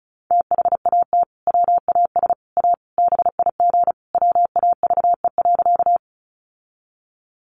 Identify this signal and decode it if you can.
Morse code, heard as a keyed tone, about 34 wpm, 700 Hz. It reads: THAT WAS A BIG WAVE.
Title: Sentences from Top 1400 Words 35wpm